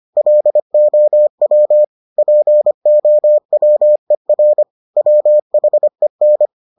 Morse code, heard as a keyed tone, about 25 wpm, 600 Hz. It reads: LOW POWER WHEN